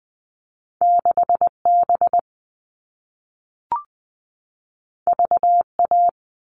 Morse code, heard as a keyed tone, about 20 wpm, 700 Hz. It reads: 6B VA
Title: All Cut Numbers - 2-Digit 20wpm